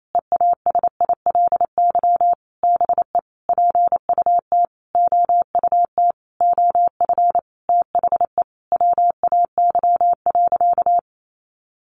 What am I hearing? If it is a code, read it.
Morse code, heard as a keyed tone, about 28 wpm, 700 Hz. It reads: EASILY BE PUT OUT OF THE WAY.
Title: Sentences from Top 900 Words 28wpm